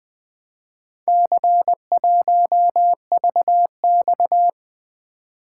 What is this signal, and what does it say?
Morse code, heard as a keyed tone, about 20 wpm, 700 Hz. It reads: C1VX
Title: Four Letter-Number Combinations - 20wpm